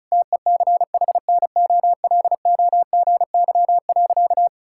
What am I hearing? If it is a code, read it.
Morse code, heard as a keyed tone, about 36 wpm, 690 Hz. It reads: TECHNOLOGY.
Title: Sentences from Top 1000 Words - Review Part 5 - 35wpm